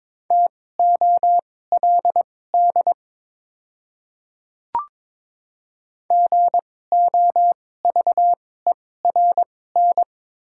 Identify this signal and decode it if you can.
Morse code, heard as a keyed tone, about 22 wpm, spaced as for 16 wpm, 700 Hz. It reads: TOLD GOVERN
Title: 2x Character Spacing - Top 500 Words - 22wpm